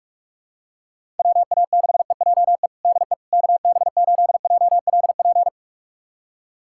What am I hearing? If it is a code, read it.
Morse code, heard as a keyed tone, about 41 wpm, 700 Hz. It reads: WA6EJE DE KB8JLP
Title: USA Callsign DE Callsign - 6 Characters - 45wpm